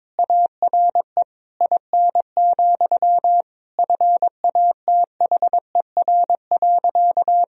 Morse code, heard as a keyed tone, about 22 wpm, 700 Hz. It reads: ARE IN, FATHER.
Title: Sentences from Top 1000 Words - Review Part 2 - 22wpm